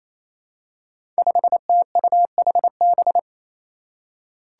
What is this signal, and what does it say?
Morse code, heard as a keyed tone, about 29 wpm, 700 Hz. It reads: 5TUHB
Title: Five Challenging Letter-Number Combinations 28wpm